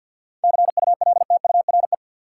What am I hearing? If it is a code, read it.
Morse code, heard as a keyed tone, about 51 wpm, 700 Hz. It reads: CULTURE
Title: Seven Letter Words - 50wpm